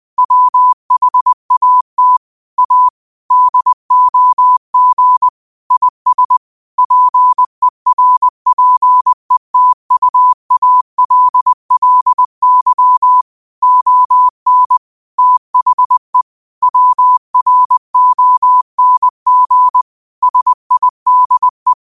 Morse code, heard as a keyed tone, about 20 wpm, 1000 Hz. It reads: WHAT A DOG IS PERPETUALLY ON THE WRONG SIDE
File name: qotdfiles / 20wpm